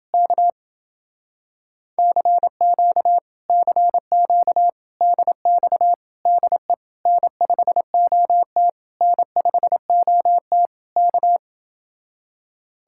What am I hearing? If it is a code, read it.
Morse code, heard as a keyed tone, about 29 wpm, 700 Hz. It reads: K CQ CQ DX DE N5OT N5OT K